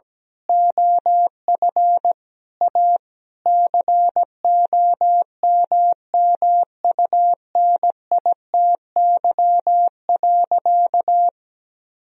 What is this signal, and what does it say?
Morse code, heard as a keyed tone, about 17 wpm, 700 Hz. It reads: OF A COMMUNITY.
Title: Sentences from Top 1000 Words - Review Part 4 - 17wpm